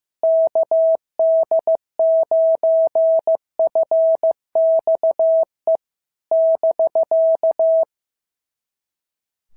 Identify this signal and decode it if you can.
Morse code, heard as a keyed tone, about 14 wpm, 650 Hz. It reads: KD9FXE <BK>
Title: SOTA QSO Practice 15wpm